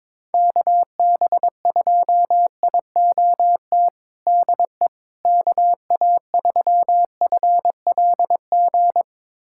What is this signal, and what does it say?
Morse code, heard as a keyed tone, about 22 wpm, 700 Hz. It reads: KB2IOT DE KA3FLG
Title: USA Callsign DE Callsign - 6 Characters - 22wpm